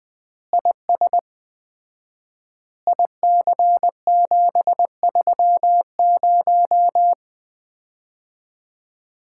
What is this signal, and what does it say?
Morse code, heard as a keyed tone, about 21 wpm, 700 Hz. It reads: IS IC730
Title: Common QSO Phrases - 4x Word-Spacing - 20wpm